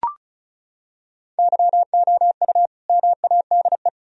Morse code, heard as a keyed tone, about 34 wpm, 700 Hz. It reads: YOU MADE
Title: Sentences from Top 1000 Words - Review Part 4 - 35wpm